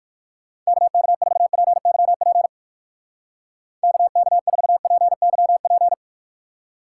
Morse code, heard as a keyed tone, about 58 wpm, 700 Hz. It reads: KK4PYP KK4PYP